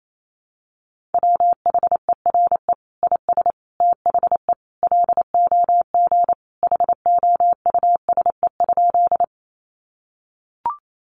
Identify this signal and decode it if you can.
Morse code, heard as a keyed tone, about 28 wpm, 700 Hz. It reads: WHERE IS THE LOG HOUSE?
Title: Sentences from Top 1400 Words 28wpm